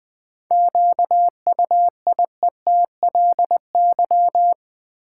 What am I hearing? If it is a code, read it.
Morse code, heard as a keyed tone, about 21 wpm, 690 Hz. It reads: QUIETLY
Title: 7 Letter Words - First 3 Letters Given - 20wpm